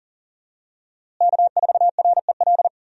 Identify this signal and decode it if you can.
Morse code, heard as a keyed tone, about 40 wpm, 700 Hz. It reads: K4REL
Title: USA Call Signs - 5 Characters - Repeated 3x - 40wpm